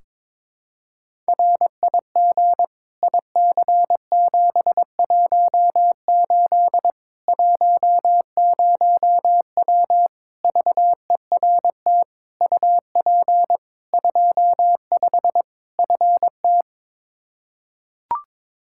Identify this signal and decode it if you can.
Morse code, heard as a keyed tone, about 22 wpm, 700 Hz. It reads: RIG IC718 10W VERT UP 25 FT
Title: Practice QSO - Exchange 2 - Info - 22wpm